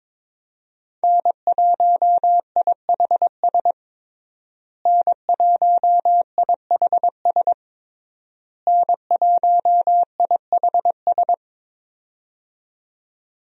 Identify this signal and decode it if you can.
Morse code, heard as a keyed tone, about 22 wpm, 700 Hz. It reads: N1IHS N1IHS N1IHS
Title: USA Call Signs - 5 Characters - Repeated 3x - 22wpm